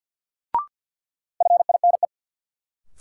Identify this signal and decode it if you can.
Morse code, heard as a keyed tone, about 52 wpm, 700 Hz. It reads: FINE
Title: Sentences from Top 500 Words - New Word Drill - 50wpm